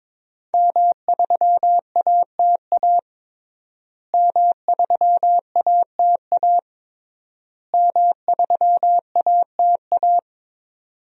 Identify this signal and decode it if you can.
Morse code, heard as a keyed tone, about 22 wpm, 700 Hz. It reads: M3ATA M3ATA M3ATA
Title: DXCC-100 Call Signs - 5 Characters - Repeated 3x - 22wpm